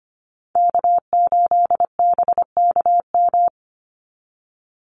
Morse code, heard as a keyed tone, about 25 wpm, 700 Hz. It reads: K8BKM
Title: Call Signs Repeated 3x - 25wpm